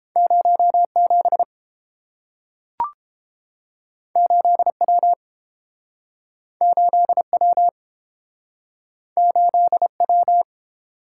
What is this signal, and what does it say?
Morse code, heard as a keyed tone, about 30 wpm, 700 Hz. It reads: 0 7 8 W 8 W 8 W